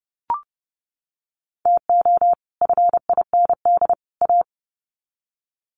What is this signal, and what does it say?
Morse code, heard as a keyed tone, about 30 wpm, 700 Hz. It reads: TO FIND A
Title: Sets of 3 Words 30wpm